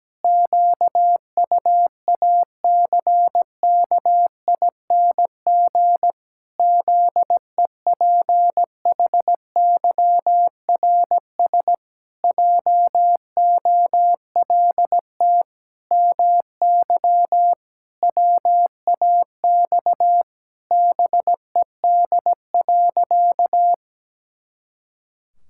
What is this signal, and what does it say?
Morse code, heard as a keyed tone, about 17 wpm, 700 Hz. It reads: QUACKING ZEPHYRS JOLT MY WAX BED.
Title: Pangram Challenge v2 - 17wpm